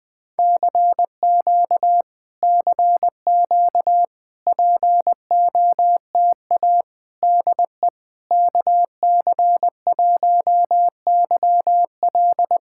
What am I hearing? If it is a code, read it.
Morse code, heard as a keyed tone, about 20 wpm, 700 Hz. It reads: CQ CQ POTA DE KC1YL
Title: POTA QSO Practice - 20wpm